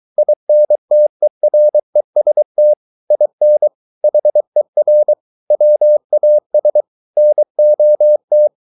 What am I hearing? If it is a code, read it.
Morse code, heard as a keyed tone, about 23 wpm, 600 Hz. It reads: INTEREST IN HER WAS NOT